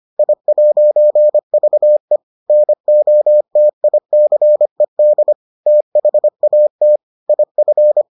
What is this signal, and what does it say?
Morse code, heard as a keyed tone, about 25 wpm, 600 Hz. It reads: I'VE NOTICED THAT IF